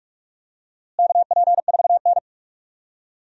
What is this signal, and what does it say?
Morse code, heard as a keyed tone, about 47 wpm, 700 Hz. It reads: KP4N
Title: DXCC-100 Call Signs - 4 Characters - Repeated 3x - 45wpm